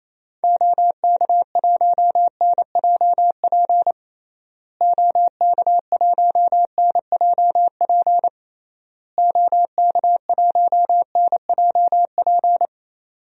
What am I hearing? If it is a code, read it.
Morse code, heard as a keyed tone, about 28 wpm, 700 Hz. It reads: OK1NJP OK1NJP OK1NJP
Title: DXCC-100 Call Signs - 6 Characters - Repeated 3x - 28wpm